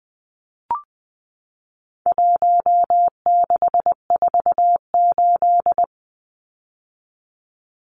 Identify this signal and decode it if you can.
Morse code, heard as a keyed tone, about 20 wpm, 700 Hz. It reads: 1648
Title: Numbers - 4-Digit 20wpm